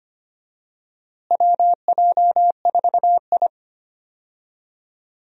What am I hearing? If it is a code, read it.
Morse code, heard as a keyed tone, about 25 wpm, 700 Hz. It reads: WJ4I
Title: USA Call Signs - 4 Characters - Repeated 3x - 25wpm